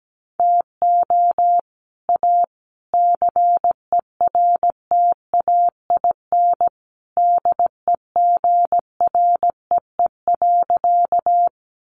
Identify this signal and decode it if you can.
Morse code, heard as a keyed tone, about 17 wpm, 700 Hz. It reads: TO A CERTAIN DEGREE.
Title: Sentences from Top 1500 Words 17wpm